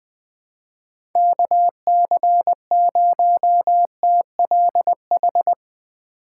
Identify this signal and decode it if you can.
Morse code, heard as a keyed tone, about 20 wpm, 700 Hz. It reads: KC0TLH
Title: USA Call Signs - 4x Character Spacing - 20wpm